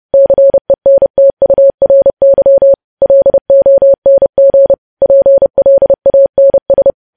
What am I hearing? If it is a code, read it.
Morse code, heard as a keyed tone, about 30 wpm, 550 Hz. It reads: CENTURY LONG PLANS
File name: podcast / News headlines-1x-30x30